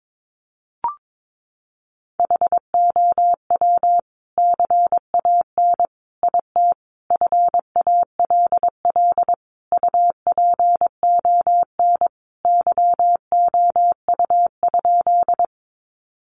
Morse code, heard as a keyed tone, about 22 wpm, 700 Hz. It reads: HOW CAN IT FALL UPON YOU?
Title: Sentences from Top 1100 Words 22wpm